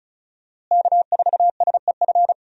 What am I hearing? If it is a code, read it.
Morse code, heard as a keyed tone, about 34 wpm, 700 Hz. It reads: K4SEF
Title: USA Call Signs - 5 Characters - Repeated 2x - 35wpm